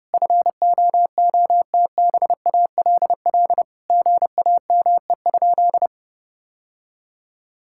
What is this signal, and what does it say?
Morse code, heard as a keyed tone, about 30 wpm, 700 Hz. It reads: FOOTBALL GAME?
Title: Sentences from Top 1600 Words 30wpm